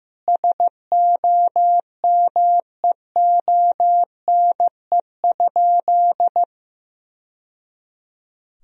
Morse code, heard as a keyed tone, about 16 wpm, 700 Hz. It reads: SOMEONE?
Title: Sentences from Top 1000 Words - Review Part 4 - 15wpm